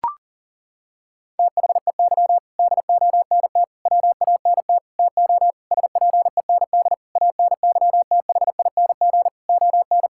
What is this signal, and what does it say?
Morse code, heard as a keyed tone, about 40 wpm, 700 Hz. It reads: THEY DONT WANT TO SPEND ANYTHING ON
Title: Sentences from Top 1000 Words - Review Part 5 - 40wpm